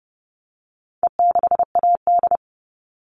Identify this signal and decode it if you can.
Morse code, heard as a keyed tone, about 30 wpm, 700 Hz. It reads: E6AD
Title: All Cut Numbers - 4-Digit 30wpm